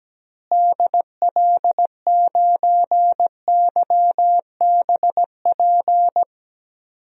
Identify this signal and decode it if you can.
Morse code, heard as a keyed tone, about 17 wpm, 700 Hz. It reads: DL9YBP
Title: DXCC-100 Call Signs - 6 Characters - Repeated 2x - 17wpm